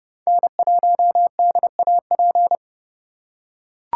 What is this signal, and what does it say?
Morse code, heard as a keyed tone, about 31 wpm, 700 Hz. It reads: N1DAP
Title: USA Call Signs - 5 Characters - 4x Character Spacing - 30wpm